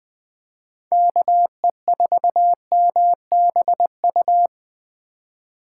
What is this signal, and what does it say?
Morse code, heard as a keyed tone, about 20 wpm, 700 Hz. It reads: KE4MBU
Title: Speed Racing - USA 2x3 Call Signs 17wpm